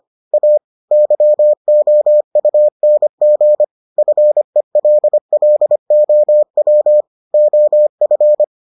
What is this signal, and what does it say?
Morse code, heard as a keyed tone, about 25 wpm, 600 Hz. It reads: A YOUNG FELLOW OF